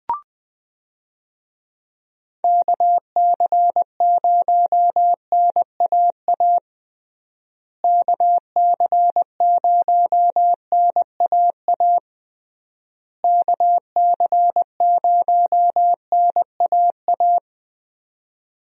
Morse code, heard as a keyed tone, about 21 wpm, 700 Hz. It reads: KC0NAA KC0NAA KC0NAA
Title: USA Call Signs - 6 Characters - Repeated 3x - 20wpm